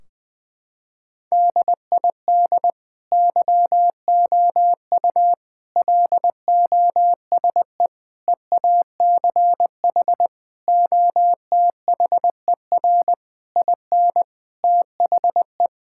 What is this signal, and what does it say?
Morse code, heard as a keyed tone, about 21 wpm, 700 Hz. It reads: DID YOU LOSE EACH OTHER IN THE
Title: Sentences from Top 1600 Words 20wpm